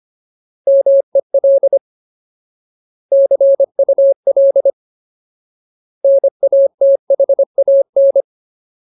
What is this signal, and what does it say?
Morse code, heard as a keyed tone, about 25 wpm, 550 Hz. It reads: MEL CUL NATHAN
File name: Ultimate Word List / 25WPM Ultimate